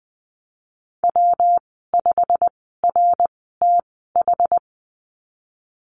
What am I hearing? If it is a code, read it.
Morse code, heard as a keyed tone, about 20 wpm, spaced as for 15 wpm, 700 Hz. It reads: W5RTH
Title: 173 Lesson 28g - Number 9 - Call Signs 20wpm